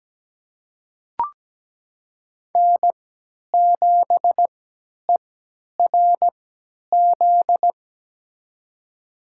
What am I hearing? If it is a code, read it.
Morse code, heard as a keyed tone, about 17 wpm, spaced as for 10 wpm, 700 Hz. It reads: N7ERZ
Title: USA Call Signs - 5 Characters - 3x Character Spacing - 17wpm